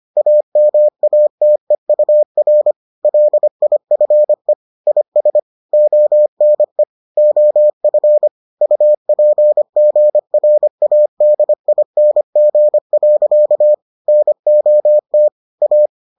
Morse code, heard as a keyed tone, about 25 wpm, 600 Hz. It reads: AMATEUR LIFE IS ONE OF UPGRADING. NOT A